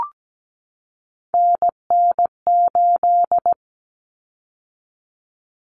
Morse code, heard as a keyed tone, about 17 wpm, 690 Hz. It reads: NN8
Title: Common Cut Numbers - 3-Digit 17wpm